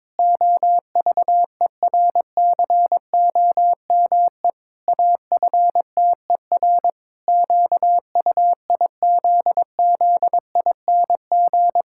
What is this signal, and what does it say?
Morse code, heard as a keyed tone, about 22 wpm, 700 Hz. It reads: OVERCOME AFTER QUIZZING
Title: Pangram Challenge v2 - 22wpm